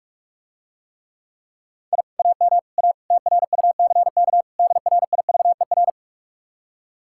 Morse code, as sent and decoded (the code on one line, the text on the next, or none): .. / .- -- / .- / - .-. ..- -.-. -.- / -.. .-. .. ...- . .-.
I AM A TRUCK DRIVER